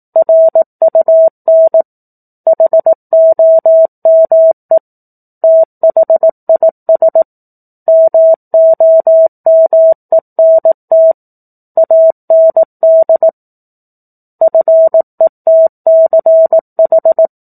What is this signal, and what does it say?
Morse code, heard as a keyed tone, about 18 wpm, 650 Hz. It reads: RUN HOME THIS MOMENT AND FETCH
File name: 04-Chapter-04 18 WPM